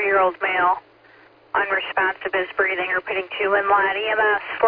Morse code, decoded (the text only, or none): none